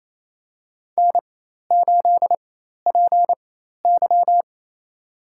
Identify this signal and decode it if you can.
Morse code, heard as a keyed tone, about 27 wpm, spaced as for 13 wpm, 700 Hz. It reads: N8PY
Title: USA Call Signs - 4 Characters - 4x Character Spacing - 28wpm